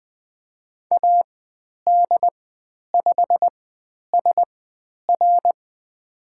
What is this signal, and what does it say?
Morse code, heard as a keyed tone, about 20 wpm, spaced as for 10 wpm, 700 Hz. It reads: AD5SR
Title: 102 Lesson 17g - Letter u - Call Signs 20wpm